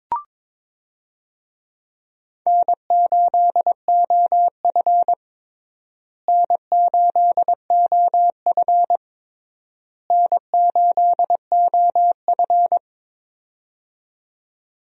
Morse code, heard as a keyed tone, about 21 wpm, 700 Hz. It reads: N8OF N8OF N8OF
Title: USA Call Signs - 4 Characters - Repeated 3x - 22wpm